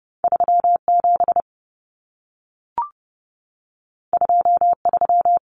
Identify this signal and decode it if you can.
Morse code, heard as a keyed tone, about 30 wpm, 700 Hz. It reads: 37 23
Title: Numbers - 2-Digit 30wpm